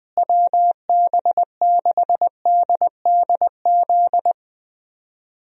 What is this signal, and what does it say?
Morse code, heard as a keyed tone, about 20 wpm, 700 Hz. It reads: WB6DDZ
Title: USA Call Signs - 6 Characters - Repeated 3x - 20wpm